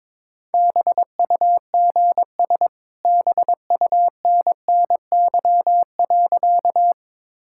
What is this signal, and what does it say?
Morse code, heard as a keyed tone, about 22 wpm, 690 Hz. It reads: BUGS BUNNY.
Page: morse-code-ninja-practice-22wpm